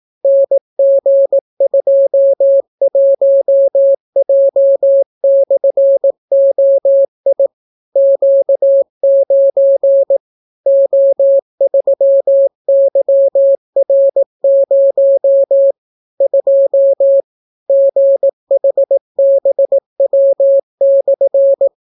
Morse code, heard as a keyed tone, about 18 wpm, 550 Hz. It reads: NG21J/OI Q9 O3YR0 2 GHBW/